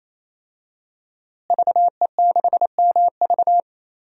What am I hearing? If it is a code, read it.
Morse code, heard as a keyed tone, about 28 wpm, 700 Hz. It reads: VE6MV
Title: DXCC-100 Call Signs - 5 Characters - Repeated 3x - 28wpm